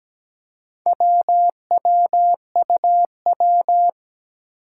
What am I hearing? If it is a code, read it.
Morse code, heard as a keyed tone, about 17 wpm, 700 Hz. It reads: WWUW
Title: Four Challenging Letter-Number Combinations 17wpm